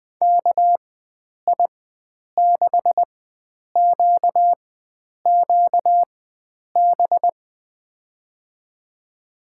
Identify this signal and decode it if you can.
Morse code, heard as a keyed tone, about 20 wpm, spaced as for 9 wpm, 700 Hz. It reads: KI6QQB
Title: USA Call Signs - 6 Characters - 4x Character Spacing - 20wpm